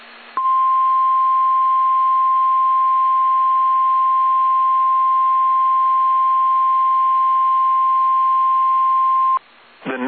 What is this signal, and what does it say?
Emergency Alert System (SAME) tones (the 1050 Hz attention tone). It is heard from 0.3 to 9.4 s.
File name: eas / KSDZKDJL / KSDZKDJL